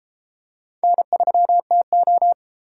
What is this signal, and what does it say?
Morse code, heard as a keyed tone, about 33 wpm, 700 Hz. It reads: N3TO